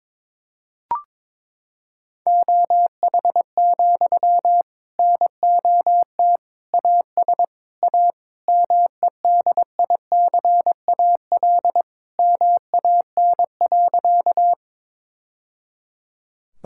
Morse code, heard as a keyed tone, about 22 wpm, 700 Hz. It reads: OH, NOT AS A MEDICAL MAN.
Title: Sentences from Top 1000 Words - Review Part 5 - 22wpm